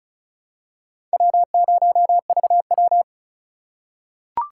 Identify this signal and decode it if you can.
Morse code, heard as a keyed tone, about 36 wpm, 700 Hz. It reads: W0VW
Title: USA Call Signs - 4 Characters - 4x Character Spacing - 35wpm